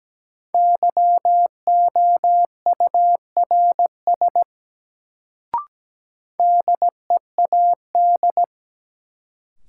Morse code, heard as a keyed tone, about 17 wpm, 700 Hz. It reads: YOURS DEAD
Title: Sentences from Top 1500 Words - New Word Drill Alt - 17wpm